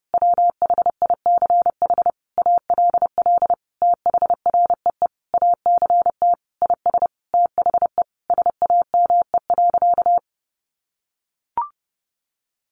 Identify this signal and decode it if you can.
Morse code, heard as a keyed tone, about 30 wpm, 700 Hz. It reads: WHICH ALL THREE ACT IS THE SAME.
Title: Sentences from Top 1400 Words 30wpm